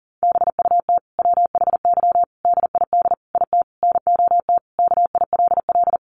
Morse code, heard as a keyed tone, about 40 wpm, 700 Hz. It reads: BUT WHY DID IT NOT KILL
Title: Sentences from Top 1400 Words 40wpm